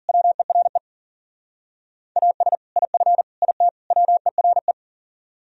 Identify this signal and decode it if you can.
Morse code, heard as a keyed tone, about 49 wpm, 700 Hz. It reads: WERE AS IF IT WERE
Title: Speed-Racing - Sets of 4 Words 40wpm